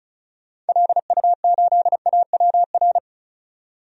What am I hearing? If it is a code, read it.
Morse code, heard as a keyed tone, about 34 wpm, 700 Hz. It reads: LU8AWR
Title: DXCC-100 Call Signs - 6 Characters - 35wpm